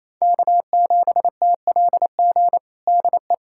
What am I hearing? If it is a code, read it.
Morse code, heard as a keyed tone, about 28 wpm, 700 Hz. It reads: K7TLG DE
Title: USA Callsign DE Callsign 28wpm